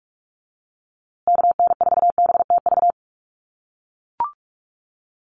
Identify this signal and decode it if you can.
Morse code, heard as a keyed tone, about 44 wpm, 700 Hz. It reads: KN4BTV
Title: USA Call Signs Repeated 2x 45wpm